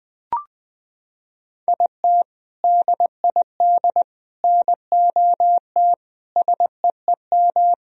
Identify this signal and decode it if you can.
Morse code, heard as a keyed tone, about 20 wpm, 700 Hz. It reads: IT DID NOT SEEM
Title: Sentences from Top 1000 Words - Review Part 2 - 20wpm